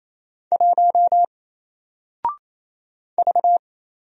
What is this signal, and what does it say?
Morse code, heard as a keyed tone, about 27 wpm, 700 Hz. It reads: 1 V
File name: Single-Letter-Number-28wpm